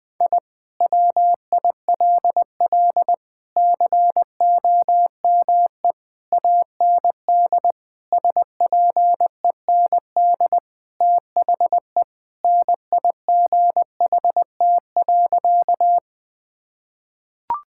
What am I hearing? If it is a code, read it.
Morse code, heard as a keyed tone, about 21 wpm, 700 Hz. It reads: I WILL COME AND SPEND THE NIGHT.
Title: Sentences from Top 1000 Words - Review Part 5 - 20wpm